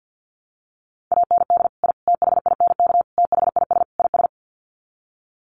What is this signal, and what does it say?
Morse code, heard as a keyed tone, about 50 wpm, 700 Hz. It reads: AND I THINK THIS IS
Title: Sets of 5 Words Encore 50wpm